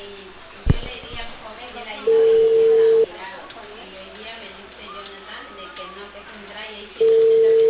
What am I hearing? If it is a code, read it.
Ringback tone (China, 450 Hz), with speech.